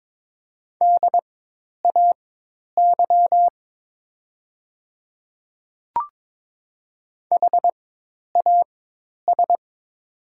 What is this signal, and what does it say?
Morse code, heard as a keyed tone, about 22 wpm, spaced as for 10 wpm, 700 Hz. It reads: DAY HAS
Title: 4x Character Spacing - Top 100 Words - 22wpm